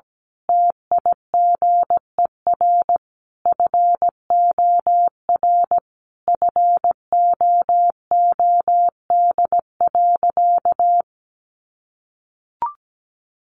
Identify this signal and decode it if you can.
Morse code, heard as a keyed tone, about 17 wpm, 700 Hz. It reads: TIGER FOR FOOD.
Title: A Sound Of Thunder 17wpm